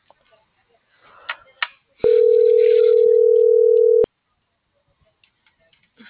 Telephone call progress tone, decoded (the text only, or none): ringback tone